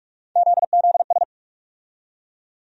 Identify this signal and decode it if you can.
Morse code, heard as a keyed tone, about 46 wpm, 700 Hz. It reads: ZZS
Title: USA Call Sign Postfixes - 3 Letters 45wpm